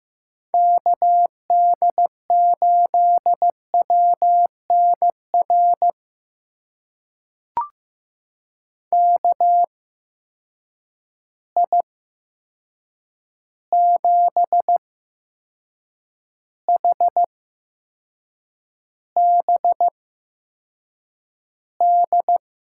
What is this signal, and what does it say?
Morse code, heard as a keyed tone, about 15 wpm, spaced as for 6 wpm, 700 Hz. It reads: KD8WNR KI7HBD
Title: USA Call Signs - 6 Characters - 8x Character Spacing - 15wpm